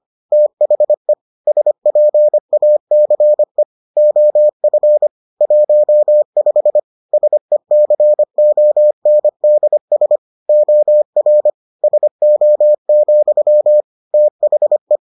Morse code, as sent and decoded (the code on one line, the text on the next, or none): - .... / ... .--. .- -.-. / --- ..-. / .---- ..... / ... -.-. --- -. -.. ... / --- .-. / ... --- --..-- / - .... .
THE SPACE OF 15 SECONDS OR SO, THE